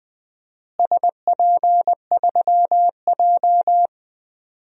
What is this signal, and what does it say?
Morse code, heard as a keyed tone, about 20 wpm, 700 Hz. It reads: SP3J
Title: DXCC-100 Call Signs - 4 Characters - Repeated 3x - 20wpm